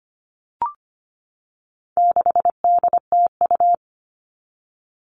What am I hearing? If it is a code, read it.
Morse code, heard as a keyed tone, about 25 wpm, 700 Hz. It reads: 6DTU
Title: All Cut Numbers - 4-Digit 25wpm